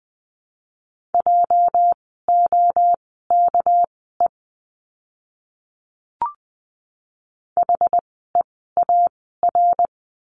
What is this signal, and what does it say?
Morse code, heard as a keyed tone, about 20 wpm, spaced as for 14 wpm, 700 Hz. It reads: JOKE HEAR
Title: 206 Lesson 33f - Letter j - Words 20wpm